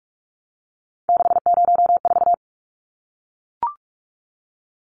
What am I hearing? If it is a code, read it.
Morse code, heard as a keyed tone, about 44 wpm, 700 Hz. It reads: 604
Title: Numbers - 3-Digit 45wpm